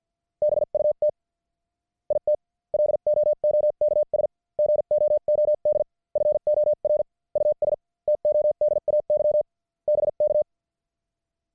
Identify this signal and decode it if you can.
Morse code, heard as a keyed tone, about 50 wpm, spaced as for 43 wpm, 600 Hz. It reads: BUT IT LOOKS GOOD FOR US TODAY BK
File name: archived / CWBros